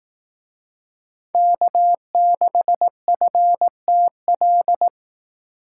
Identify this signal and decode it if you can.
Morse code, heard as a keyed tone, about 18 wpm, 700 Hz. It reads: K6FTL
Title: Speed Racing - USA 1x3 Call Signs 15wpm